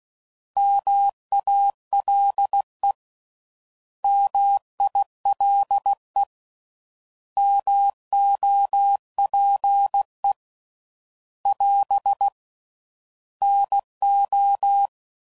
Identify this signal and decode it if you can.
Morse code, heard as a keyed tone, about 16 wpm, 800 Hz. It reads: MALE MILE MOPE <AS> NO